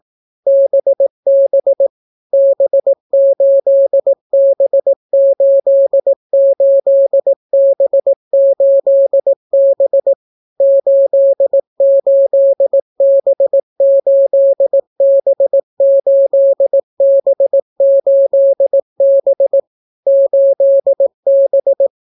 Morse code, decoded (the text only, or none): BB B8B88B8B 88B8B8B8B 8B